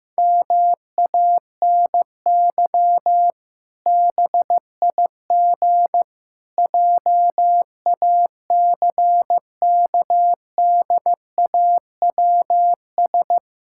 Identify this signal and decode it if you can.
Morse code, heard as a keyed tone, about 15 wpm, 700 Hz. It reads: MANY BIG JACKDAWS
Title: Pangram Challenge v2 - 15wpm